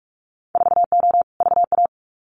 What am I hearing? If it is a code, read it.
Morse code, heard as a keyed tone, about 46 wpm, 700 Hz. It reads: 4O VA
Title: Winter Field Day Exchanges Encore 45wpm